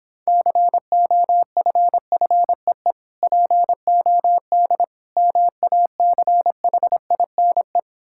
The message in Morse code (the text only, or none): COFFEE POD MACHINE